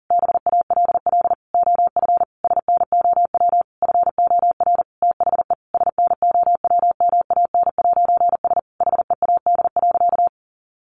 Morse code, heard as a keyed tone, about 40 wpm, 700 Hz. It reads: BALL OF SNOW FOR THE SNOWMAN'S HEAD.
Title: Easy Phrases 40wpm